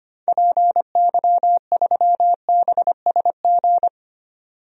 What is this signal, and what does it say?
Morse code, heard as a keyed tone, about 25 wpm, 700 Hz. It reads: PY3BSG
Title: DXCC-100 Call Signs - 6 Characters - 25wpm